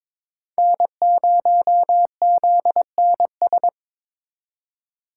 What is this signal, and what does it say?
Morse code, heard as a keyed tone, about 22 wpm, 700 Hz. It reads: N0ZNS
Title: USA Call Signs - 4x Character Spacing - 22wpm